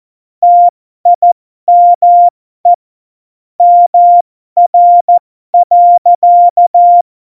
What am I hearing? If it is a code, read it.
Morse code, heard as a keyed tone, about 14 wpm, spaced as for 12 wpm, 700 Hz. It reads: TIME MR.
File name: Morsekurs / Witze